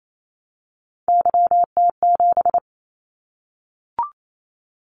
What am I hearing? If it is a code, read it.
Morse code, heard as a keyed tone, about 28 wpm, 700 Hz. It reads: YT7
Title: Callsign Prefixes for Top 25 DXCC - 28wpm